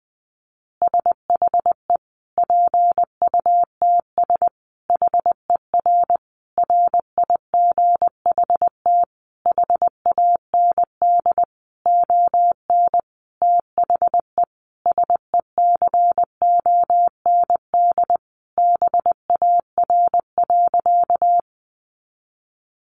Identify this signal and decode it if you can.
Morse code, heard as a keyed tone, about 20 wpm, 700 Hz. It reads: SHE PUTS HER RIGHT HAND ON THE SECOND BAR.
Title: Easy Phrases 20wpm